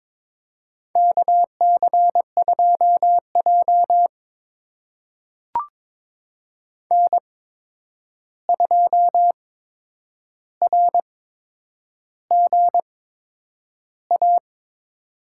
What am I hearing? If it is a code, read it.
Morse code, heard as a keyed tone, about 22 wpm, spaced as for 9 wpm, 700 Hz. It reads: KC2J N2RGA
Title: USA Call Signs - 8x Character Spacing - 22wpm